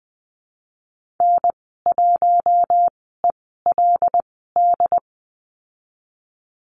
Morse code, heard as a keyed tone, about 20 wpm, spaced as for 14 wpm, 700 Hz. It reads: N1ELD
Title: 088 Lesson 15g - Number 5 - Call Signs 20wpm